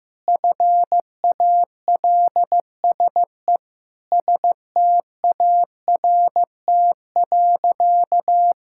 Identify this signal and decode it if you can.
Morse code, heard as a keyed tone, about 15 wpm, 700 Hz. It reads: FALSE START.
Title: Sentences from Top 1600 Words 15wpm